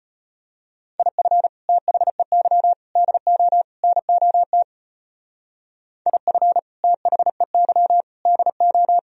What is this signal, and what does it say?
Morse code, heard as a keyed tone, about 30 wpm, 700 Hz. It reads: IF THEY DO NOT IF THEY DO
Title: Speed-Racing - Sets of 4 Words 25wpm